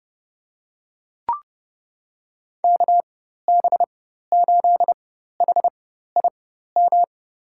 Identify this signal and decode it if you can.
Morse code, heard as a keyed tone, about 30 wpm, spaced as for 14 wpm, 700 Hz. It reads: KB8HIM
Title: USA Call Signs - 4x Character Spacing - 30wpm